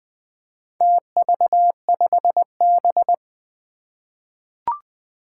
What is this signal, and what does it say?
Morse code, heard as a keyed tone, about 20 wpm, 700 Hz. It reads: TV5B
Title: Four Challenging Letter-Number Combinations 20wpm